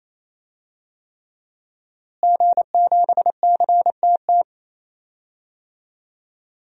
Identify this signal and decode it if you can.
Morse code, heard as a keyed tone, about 28 wpm, 700 Hz. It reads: G7CTT
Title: DXCC-100 Call Signs - 5 Characters - Repeated 3x - 28wpm